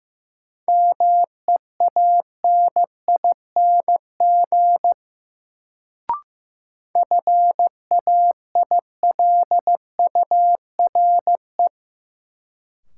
Morse code, heard as a keyed tone, about 16 wpm, 700 Hz. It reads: MEANING FAILURE
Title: Sentences from Top 900 Words - New Word Drill - 15wpm